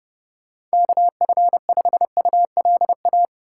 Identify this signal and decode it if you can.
Morse code, heard as a keyed tone, about 30 wpm, 700 Hz. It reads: KF5ULA
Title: USA Call Signs - 6 Characters - 3x Character Spacing - 30wpm